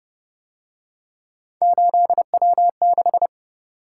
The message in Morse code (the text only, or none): 8W6